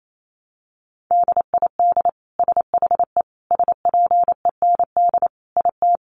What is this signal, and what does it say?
Morse code, heard as a keyed tone, about 28 wpm, 700 Hz. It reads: DID SHE SPEND IT
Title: Sentences from Top 800 Words 28wpm